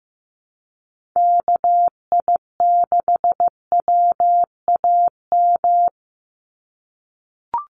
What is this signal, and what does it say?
Morse code, heard as a keyed tone, about 15 wpm, 700 Hz. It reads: KI6WAM
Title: USA Call Signs 15wpm